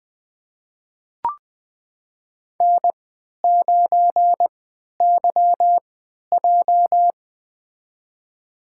Morse code, heard as a keyed tone, about 20 wpm, spaced as for 11 wpm, 700 Hz. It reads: N9YJ
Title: USA Call Signs - 4 Characters - 3x Character Spacing - 20wpm